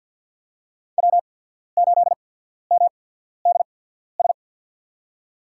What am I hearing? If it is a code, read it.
Morse code, heard as a keyed tone, about 53 wpm, spaced as for 14 wpm, 700 Hz. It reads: W8MDS
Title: USA Call Signs - 5 Characters - 8x Character Spacing - 50wpm